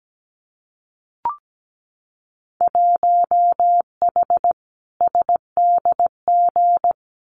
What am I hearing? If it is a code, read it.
Morse code, heard as a keyed tone, about 17 wpm, 700 Hz. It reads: 1H SDG
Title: Winter Field Day Exchanges 17wpm